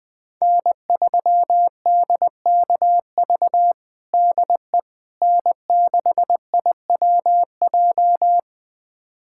Morse code, heard as a keyed tone, about 21 wpm, 700 Hz. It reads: N3DKV DE N6IWJ
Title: USA Callsign DE Callsign - 5 Characters - 20wpm